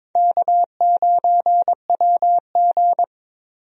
Morse code, heard as a keyed tone, about 22 wpm, 700 Hz. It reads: K9WG
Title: USA Call Signs - 4 Characters - 22wpm